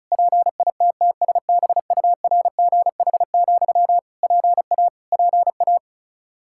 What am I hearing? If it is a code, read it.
Morse code, heard as a keyed tone, about 36 wpm, 700 Hz. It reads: PITTSBURGH, PA PA
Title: Practice QSO - Exchange 1 - Beginning - 35wpm